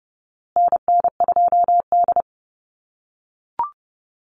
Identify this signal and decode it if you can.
Morse code, heard as a keyed tone, about 30 wpm, 700 Hz. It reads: NN2D
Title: USA Call Signs Repeated 2x 30wpm